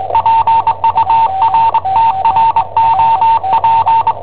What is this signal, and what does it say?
Morse code, heard as a keyed tone, about 20 wpm, 910 Hz, with noise. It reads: PURTROP
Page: Cassetta esercizi preparata da I7OHP per IW5ELP